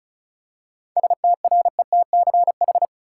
Morse code, heard as a keyed tone, about 36 wpm, 700 Hz. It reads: STRETCH
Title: Seven Letter Words - 35wpm